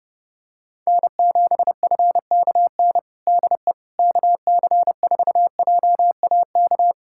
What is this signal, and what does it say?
Morse code, heard as a keyed tone, about 30 wpm, 700 Hz. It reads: N7FKN DE KC4JAK
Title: Practice QSO - Exchange 4 - Ending - 30wpm